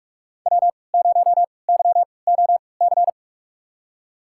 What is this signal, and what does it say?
Morse code, heard as a keyed tone, about 46 wpm, spaced as for 25 wpm, 700 Hz. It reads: W0YOC
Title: USA Call Signs - 5 Characters - 3x Character Spacing - 45wpm